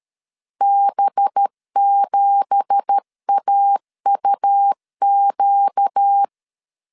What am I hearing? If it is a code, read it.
Morse code, heard as a keyed tone, about 13 wpm, 800 Hz. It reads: B7AUQ
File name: CW / DET / 1 9 1500C S B 12wpm 3819